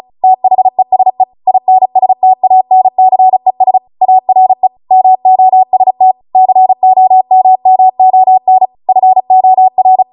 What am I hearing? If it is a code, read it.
Morse code, heard as a keyed tone, about 36 wpm, 750 Hz. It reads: THESE INSTANCES ARE MOST COMMON FOR